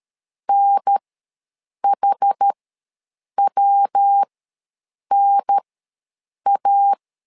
Morse code, heard as a keyed tone, about 13 wpm, spaced as for 7 wpm, 800 Hz. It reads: NHWNA
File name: LEZIONI / 3 MWRO / 3 MW ws 12wpm 1816